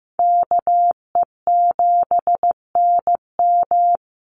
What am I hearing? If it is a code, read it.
Morse code, heard as a keyed tone, about 15 wpm, 700 Hz. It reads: KE7NM
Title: USA Call Signs 15wpm